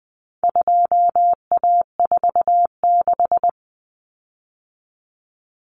Morse code, heard as a keyed tone, about 20 wpm, 700 Hz. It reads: 2A46
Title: Common Cut Numbers - 4-Digit 20wpm